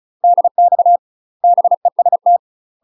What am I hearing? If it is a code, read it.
Morse code, heard as a keyed tone, about 35 wpm, 700 Hz. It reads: DX BEST
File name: UK MASTER QSOs TRAINING / 35WPM / UK MASTER QSOs 0014